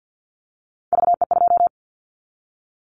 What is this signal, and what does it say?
Morse code, heard as a keyed tone, about 50 wpm, 700 Hz. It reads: VE2
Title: Callsign Prefixes for Top 50 DXCC - 50wpm